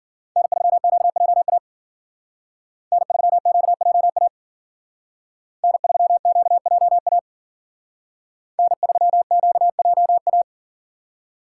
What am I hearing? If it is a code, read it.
Morse code, heard as a keyed tone, about 50 wpm, 700 Hz. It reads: N3QJA N3QJA N3QJA N3QJA